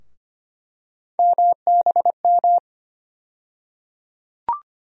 Morse code, heard as a keyed tone, about 25 wpm, 700 Hz. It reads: MBM
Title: USA Call Sign Postfixes - 3 Letters 25wpm